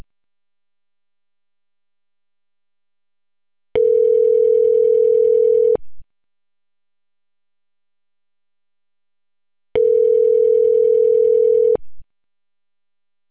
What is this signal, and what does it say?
Ringback tone (North America), 440 + 480 Hz.